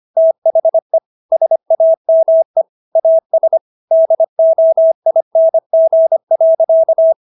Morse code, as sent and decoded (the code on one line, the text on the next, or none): - .... / ... .- -- . / .- ... / -.. --- .. -. --. .-.-.-
THE SAME AS DOING.